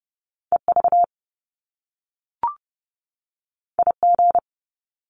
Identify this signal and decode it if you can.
Morse code, heard as a keyed tone, about 30 wpm, 700 Hz. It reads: EV IG
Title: Top Two Letter Combinations in Words 30wpm